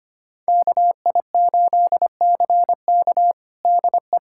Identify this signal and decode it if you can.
Morse code, heard as a keyed tone, about 25 wpm, 700 Hz. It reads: KI8CK DE
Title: USA Callsign DE Callsign - 5 Characters - 25wpm